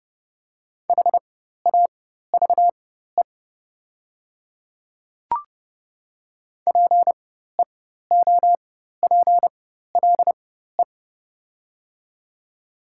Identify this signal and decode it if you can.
Morse code, heard as a keyed tone, about 31 wpm, spaced as for 14 wpm, 690 Hz. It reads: HAVE PEOPLE
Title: 4x Character Spacing - Top 100 Words - 30wpm